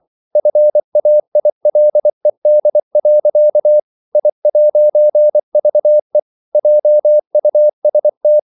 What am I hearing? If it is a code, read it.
Morse code, heard as a keyed tone, about 24 wpm, 600 Hz. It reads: FAILED. I'VE JUST
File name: CW / Practice / EnglishQuotations0224